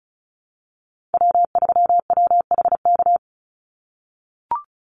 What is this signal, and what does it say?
Morse code, heard as a keyed tone, about 34 wpm, 700 Hz. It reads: W3WHK
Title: Call Signs - 35wpm